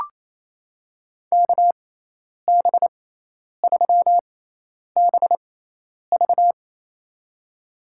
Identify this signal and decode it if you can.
Morse code, heard as a keyed tone, about 28 wpm, spaced as for 10 wpm, 700 Hz. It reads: KB3BV
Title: USA Call Signs - 5 Characters - 6x Character Spacing - 28wpm